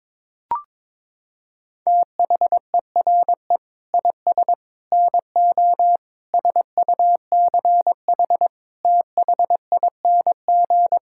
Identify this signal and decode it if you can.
Morse code, heard as a keyed tone, about 21 wpm, 700 Hz. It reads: THERE IS NO SUCH THING
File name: Sentences with 7-Word Phrases - 22wpm